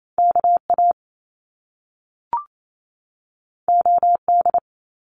Morse code, heard as a keyed tone, about 28 wpm, 700 Hz. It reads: KA OD
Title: Top Two Letter Combinations in Words 28wpm